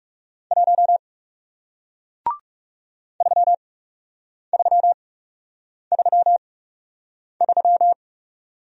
Morse code, heard as a keyed tone, about 39 wpm, spaced as for 8 wpm, 700 Hz. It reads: 1 3333